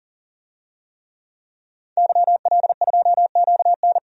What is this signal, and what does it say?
Morse code, heard as a keyed tone, about 40 wpm, 700 Hz. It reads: YL2QN
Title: DXCC-100 Call Signs - 5 Characters - 40wpm